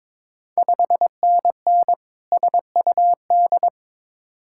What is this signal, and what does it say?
Morse code, heard as a keyed tone, about 21 wpm, 700 Hz. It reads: 5NN SUD
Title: Ontario QSO Party Exchange 22wpm